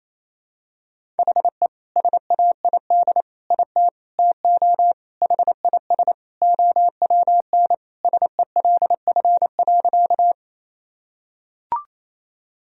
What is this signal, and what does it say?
Morse code, heard as a keyed tone, about 27 wpm, 700 Hz. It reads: HE SAID IT TO HIS OWN SELF.
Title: Sentences from Top 1000 Words - Review Part 1 - 28wpm